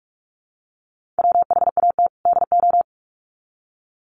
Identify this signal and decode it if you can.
Morse code, heard as a keyed tone, about 44 wpm, 700 Hz. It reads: WHAT DO
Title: Sets of 2 Words 45wpm